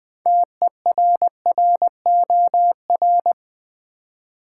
Morse code, heard as a keyed tone, about 20 wpm, 690 Hz. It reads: TERROR
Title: Build A Word - 2 Letters At A Time - 6-Letter Words - 20wpm